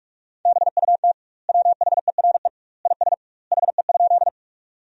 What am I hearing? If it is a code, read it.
Morse code, heard as a keyed tone, about 46 wpm, 700 Hz. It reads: BUT WHERE IS HE?
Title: 2x Word-Spacing - Sentences from Top 200 Words - 45wpm